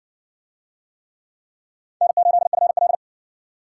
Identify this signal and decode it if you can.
Morse code, heard as a keyed tone, about 60 wpm, 700 Hz. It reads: N8FL